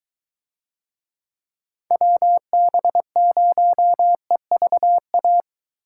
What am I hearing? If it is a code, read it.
Morse code, heard as a keyed tone, about 21 wpm, 700 Hz. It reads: WB0EVA